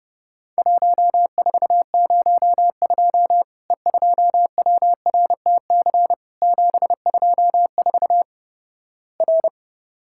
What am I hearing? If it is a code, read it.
Morse code, heard as a keyed tone, about 30 wpm, 700 Hz. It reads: 1402 E2WRTC 724 R
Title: Worked All Europe DX Contest - QTC Exchange - 30wpm